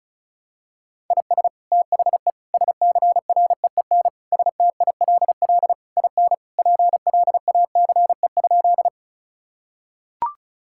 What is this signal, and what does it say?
Morse code, heard as a keyed tone, about 34 wpm, 700 Hz. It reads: IS THE SCREEN STILL IN PLACE?
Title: Sentences from Top 1000 Words - Review Part 6 - 35wpm